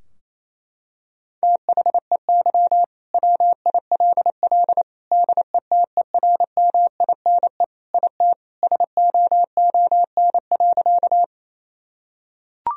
Morse code, heard as a keyed tone, about 28 wpm, 700 Hz. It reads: THEY WILL DETERMINE IT SOON.
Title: Sentences from Top 1000 Words - Review Part 4 - 28wpm